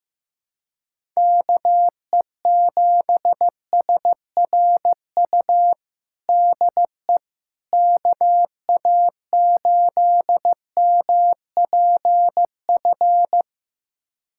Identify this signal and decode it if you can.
Morse code, heard as a keyed tone, about 15 wpm, 700 Hz. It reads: KE7SRU DE KA8MPF
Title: USA Callsign DE Callsign 15wpm